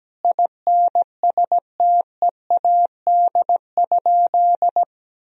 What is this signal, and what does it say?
Morse code, heard as a keyed tone, about 17 wpm, 700 Hz. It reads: INSTEAD?
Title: Sentences from Top 1000 Words - Review Part 4 - 17wpm